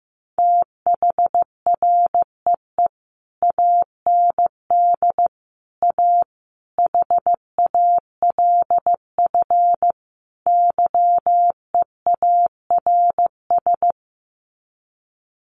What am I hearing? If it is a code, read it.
Morse code, heard as a keyed tone, about 15 wpm, 700 Hz. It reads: THREE AND A HALF YEARS
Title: Sets of 5 Words Encore2 15wpm